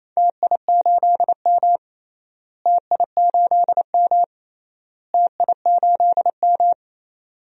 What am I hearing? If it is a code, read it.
Morse code, heard as a keyed tone, about 27 wpm, 700 Hz. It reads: TI8M TI8M TI8M
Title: DXCC-100 Call Signs - 4 Characters - Repeated 3x - 28wpm